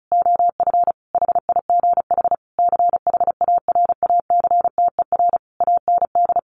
Morse code, heard as a keyed tone, about 34 wpm, 700 Hz. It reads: OF HIGH CHARACTER AND
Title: Sentences from Top 1200 Words 35wpm